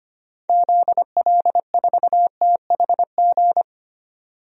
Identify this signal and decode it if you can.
Morse code, heard as a keyed tone, about 25 wpm, 700 Hz. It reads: ZL4THG
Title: DXCC-100 Call Signs - 6 Characters - 25wpm